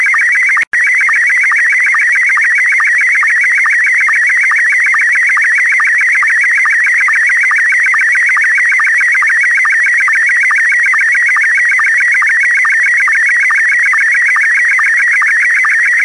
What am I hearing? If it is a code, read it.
SSTV (slow-scan television), Scottie 1.